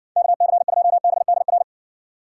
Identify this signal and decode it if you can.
Morse code, heard as a keyed tone, about 58 wpm, 700 Hz. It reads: KC2BDR